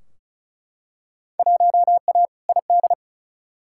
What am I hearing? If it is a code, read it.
Morse code, heard as a keyed tone, about 36 wpm, 700 Hz. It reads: 1A ID
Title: Field Day Exchanges - 35wpm